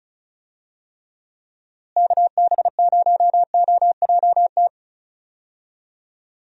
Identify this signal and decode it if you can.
Morse code, heard as a keyed tone, about 36 wpm, 700 Hz. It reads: KB0OJT
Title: USA Call Signs - 6 Characters - Repeated 2x - 35wpm